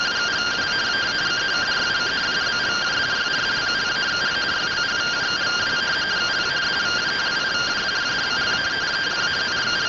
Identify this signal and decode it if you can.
Two-tone teleprinter signal, about 170 Hz shift.